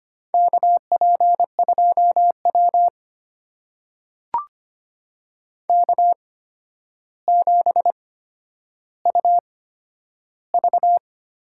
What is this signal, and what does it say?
Morse code, heard as a keyed tone, about 25 wpm, spaced as for 11 wpm, 700 Hz. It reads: KP2W K7UV
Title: USA Call Signs - 4 Characters - 8x Character Spacing - 25wpm